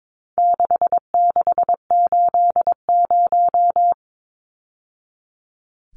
Morse code, heard as a keyed tone, about 22 wpm, 700 Hz. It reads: 6680
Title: Numbers - 4-Digit 22wpm